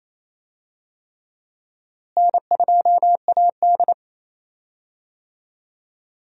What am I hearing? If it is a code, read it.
Morse code, heard as a keyed tone, about 28 wpm, 700 Hz. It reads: N2AD
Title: USA Call Signs - 4 Characters - Repeated 3x - 28wpm